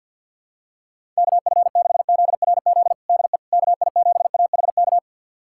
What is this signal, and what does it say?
Morse code, heard as a keyed tone, about 52 wpm, 700 Hz. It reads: KF6ZRZ DE KI7AHK
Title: USA Callsign DE Callsign 50wpm